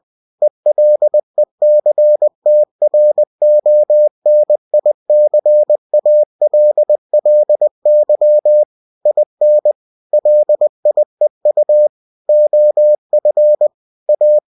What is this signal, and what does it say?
Morse code, heard as a keyed tone, about 20 wpm, 600 Hz. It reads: ELECTRONICALLY IN LIEU OF A
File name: cw-books / winnie-the-pooh / winnie-20wpm-page-0119